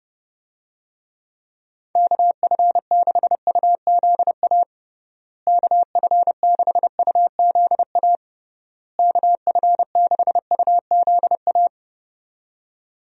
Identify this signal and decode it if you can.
Morse code, heard as a keyed tone, about 30 wpm, 700 Hz. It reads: KF6UZA KF6UZA KF6UZA
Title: USA Call Signs - 6 Characters - Repeated 3x - 30wpm